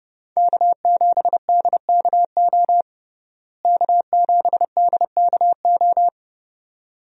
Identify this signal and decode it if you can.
Morse code, heard as a keyed tone, about 30 wpm, 700 Hz. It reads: K7DKO K7DKO
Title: USA Call Signs - 5 Characters - Repeated 2x - 30wpm